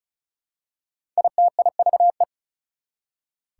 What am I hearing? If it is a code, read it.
Morse code, heard as a keyed tone, about 36 wpm, 700 Hz. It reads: ITIVE
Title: Prefixes and Suffixes 35wpm